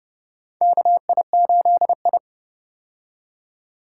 Morse code, heard as a keyed tone, about 30 wpm, 700 Hz. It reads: KI8I
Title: K1USN SST QSO Practice - 30wpm